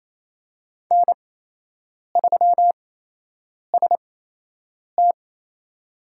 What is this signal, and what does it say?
Morse code, heard as a keyed tone, about 28 wpm, spaced as for 8 wpm, 700 Hz. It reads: N3ST
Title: USA Call Signs - 5 Characters - 8x Character Spacing - 28wpm